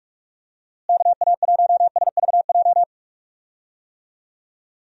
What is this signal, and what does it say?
Morse code, heard as a keyed tone, about 46 wpm, 700 Hz. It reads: KA1SVJ
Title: USA Call Signs - 6 Characters - Repeated 2x - 45wpm